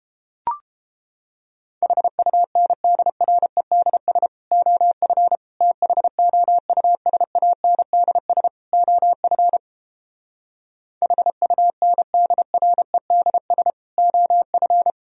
Morse code, heard as a keyed tone, about 29 wpm, 700 Hz. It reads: HUNDREDS OF THOUSANDS OF HUNDREDS OF